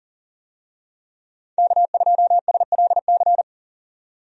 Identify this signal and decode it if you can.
Morse code, heard as a keyed tone, about 40 wpm, 700 Hz. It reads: K2SLC
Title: USA Call Signs - 5 Characters - 40wpm